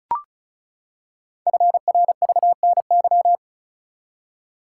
Morse code, heard as a keyed tone, about 36 wpm, 700 Hz. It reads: FRVNY
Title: Code Groups - 5 Letters - 35wpm